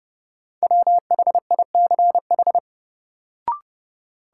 Morse code, heard as a keyed tone, about 30 wpm, 700 Hz. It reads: WHICH
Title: Sentences from Top 100 Words - New Word Drill - 30wpm